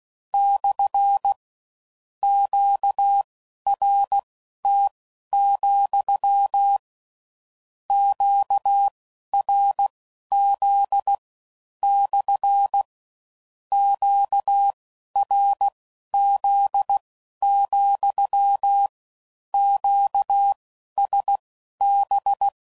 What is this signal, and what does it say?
Morse code, heard as a keyed tone, about 16 wpm, spaced as for 11 wpm, 800 Hz. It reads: / QRT, QRZ / QRZ, QSB
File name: CW Code Groups @15WPM / Q Calls / Q Calls SPSL2